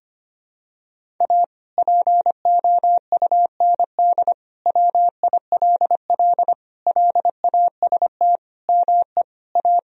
Morse code, heard as a keyed tone, about 25 wpm, 700 Hz. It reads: A POUND WILL LAST ME A
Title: Sentences from Top 1000 Words - Review Part 2 - 25wpm